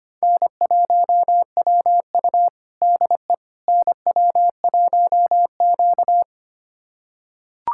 Morse code, heard as a keyed tone, about 25 wpm, 700 Hz. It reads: N1WU DE NW1Q
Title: USA Callsign DE Callsign - 4 Characters - 25wpm